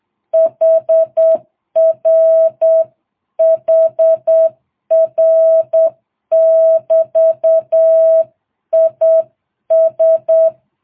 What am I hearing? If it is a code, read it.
Morse code, heard as a keyed tone, about 8 wpm, 660 Hz. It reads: HR HR <BT> IS